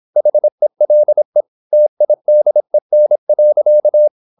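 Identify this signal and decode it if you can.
Morse code, heard as a keyed tone, about 26 wpm, 600 Hz. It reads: HELE TIDEN.
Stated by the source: High-speed (QRQ) Morse at 26 wpm.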